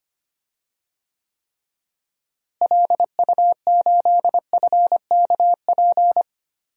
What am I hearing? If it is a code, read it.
Morse code, heard as a keyed tone, about 25 wpm, 700 Hz. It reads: LU8FKP
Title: DXCC-100 Call Signs - 6 Characters - Repeated 3x - 25wpm